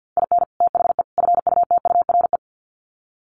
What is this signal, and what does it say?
Morse code, heard as a keyed tone, about 50 wpm, 700 Hz. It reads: IN THE FUTURE
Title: Sets of 3 Words 50wpm